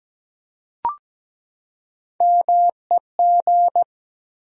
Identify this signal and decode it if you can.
Morse code, heard as a keyed tone, about 17 wpm, 700 Hz. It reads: MEG
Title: USA Common Names - Max 3 Letters - 17wpm